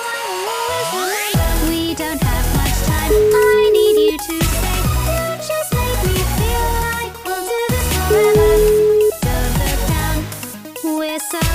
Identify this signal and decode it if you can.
Ringback tone (Europe, 425 Hz), with noise.